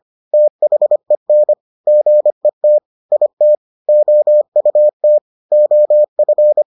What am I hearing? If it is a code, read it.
Morse code, heard as a keyed tone, about 25 wpm, 600 Hz. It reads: THEN GET IT OUT OF